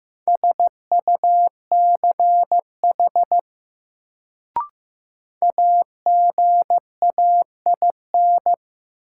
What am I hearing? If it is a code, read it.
Morse code, heard as a keyed tone, about 15 wpm, 700 Hz. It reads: SUCH AGAIN
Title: Sentences from Top 200 Words - New Word Drill - 15wpm